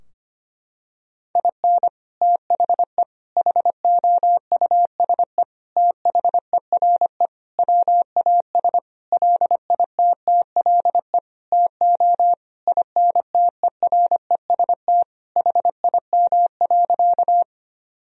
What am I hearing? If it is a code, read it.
Morse code, heard as a keyed tone, about 25 wpm, 700 Hz. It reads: IN THE HOUSE THERE WAS LITTLE TO INTEREST HIM.
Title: Sentences from Top 1000 Words - Review Part 2 - 25wpm